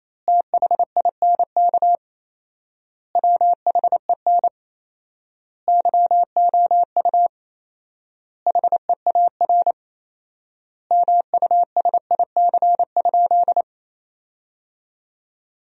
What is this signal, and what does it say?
Morse code, heard as a keyed tone, about 28 wpm, 700 Hz. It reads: THINK WHEN YOU HEAR MUSIC?
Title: 4x Word-Spacing - Sentences from Top 300 Words - 28wpm